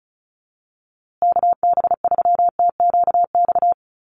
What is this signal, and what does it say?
Morse code, heard as a keyed tone, about 34 wpm, 700 Hz. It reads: KB3TQX
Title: USA Call Signs Repeated 2x 35wpm